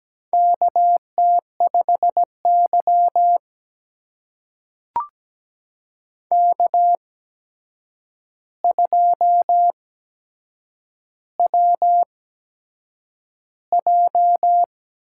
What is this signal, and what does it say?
Morse code, heard as a keyed tone, about 17 wpm, spaced as for 7 wpm, 700 Hz. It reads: KT5Y K2WJ
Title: USA Call Signs - 4 Characters - 8x Character Spacing - 17wpm